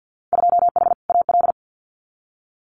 Morse code, heard as a keyed tone, about 50 wpm, 700 Hz. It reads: 2H AL
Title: Winter Field Day Exchanges 50wpm